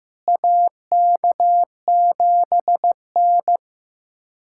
Morse code, heard as a keyed tone, about 15 wpm, 700 Hz. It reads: AK7N
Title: USA Callsign DE Callsign - 4 Characters - 15wpm